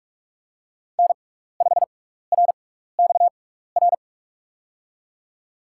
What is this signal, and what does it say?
Morse code, heard as a keyed tone, about 44 wpm, spaced as for 15 wpm, 700 Hz. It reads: N5RXR
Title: USA Call Signs - 5 Characters - 6x Character Spacing - 45wpm